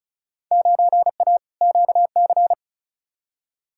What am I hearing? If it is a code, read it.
Morse code, heard as a keyed tone, about 36 wpm, 700 Hz. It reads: 9A QC
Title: Field Day Exchanges Encore - 35wpm